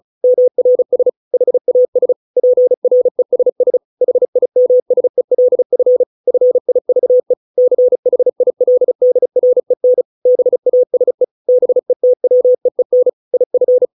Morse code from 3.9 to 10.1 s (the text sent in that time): HIMSELF FIVE CHILDREN